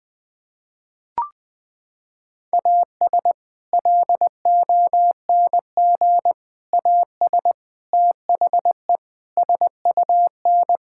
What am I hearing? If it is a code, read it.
Morse code, heard as a keyed tone, about 21 wpm, 700 Hz. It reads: AS LONG AS THE SUN
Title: Sentences with 4-Word Phrases - 20wpm